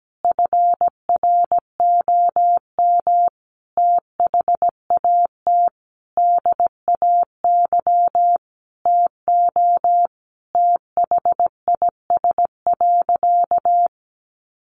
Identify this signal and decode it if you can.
Morse code, heard as a keyed tone, about 17 wpm, 700 Hz. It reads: FROM THAT DAY TO THIS.
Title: Sentences from Top 1400 Words 17wpm